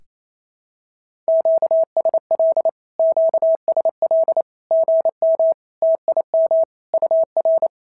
Morse code, heard as a keyed tone, about 28 wpm, 650 Hz. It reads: QSL QSL GM TIM UR
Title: SOTA QSO Practice 28wpm